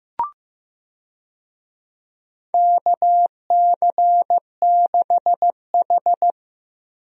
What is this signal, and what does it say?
Morse code, heard as a keyed tone, about 16 wpm, 700 Hz. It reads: KC6H
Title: USA Call Signs - 4 Characters - Repeated 2x - 15wpm